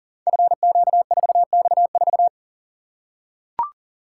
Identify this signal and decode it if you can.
Morse code, heard as a keyed tone, about 38 wpm, 690 Hz. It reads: FQ4X4
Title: Code Groups - 5 Letter:Numbers - 40wpm